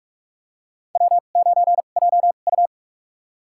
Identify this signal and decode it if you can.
Morse code, heard as a keyed tone, about 46 wpm, spaced as for 33 wpm, 700 Hz. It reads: W9JU
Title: USA Call Signs - 4 Characters - 2x Character Spacing - 45wpm